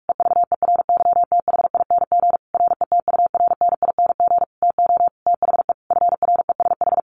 Morse code, heard as a keyed tone, about 46 wpm, 700 Hz. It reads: EVERYTHING RETURNING TO THE FRESH
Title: A Sound Of Thunder 45wpm